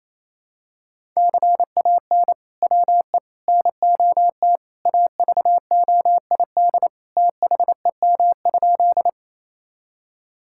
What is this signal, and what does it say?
Morse code, heard as a keyed tone, about 28 wpm, 700 Hz. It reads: CAN WE NOT AVOID THEM?
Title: Sentences from Top 1000 Words - Review Part 6 - 28wpm